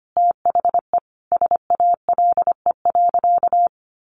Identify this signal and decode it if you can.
Morse code, heard as a keyed tone, about 25 wpm, 700 Hz. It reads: THE SALE.
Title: Sentences from Top 1400 Words 25wpm